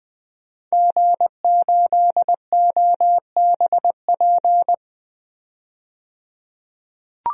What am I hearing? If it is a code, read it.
Morse code, heard as a keyed tone, about 20 wpm, 700 Hz. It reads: G8OBP
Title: DXCC-100 Call Signs - 5 Characters - 20wpm